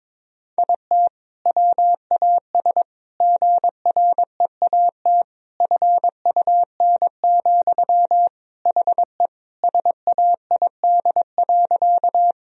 Morse code, heard as a keyed tone, about 22 wpm, 700 Hz. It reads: IT WAS GREAT FUN, HE SAID.
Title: Sentences from Top 1000 Words - Review Part 4 - 22wpm